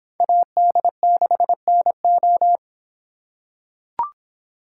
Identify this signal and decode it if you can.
Morse code, heard as a keyed tone, about 24 wpm, 700 Hz. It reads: AD6NO